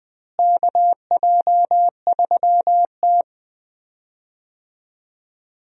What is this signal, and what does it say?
Morse code, heard as a keyed tone, about 20 wpm, 700 Hz. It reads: KJ3T
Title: USA Call Signs - 4 Characters - Repeated 3x - 20wpm